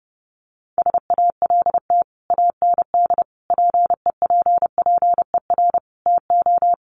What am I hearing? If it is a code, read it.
Morse code, heard as a keyed tone, about 28 wpm, 700 Hz. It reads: SALT AND PEPPER TO
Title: Sets of 5 Words 30wpm